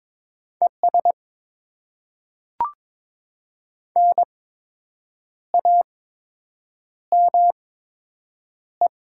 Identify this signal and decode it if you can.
Morse code, heard as a keyed tone, about 22 wpm, spaced as for 7 wpm, 700 Hz. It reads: ES NAME
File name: Top 20 QSO Elements - 8x Character Spacing - 22wpm